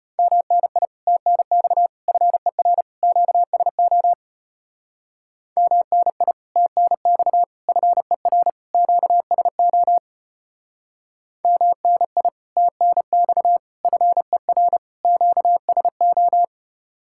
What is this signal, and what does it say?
Morse code, heard as a keyed tone, about 33 wpm, 700 Hz. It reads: MNI TNX FER QSO MNI TNX FER QSO MNI TNX FER QSO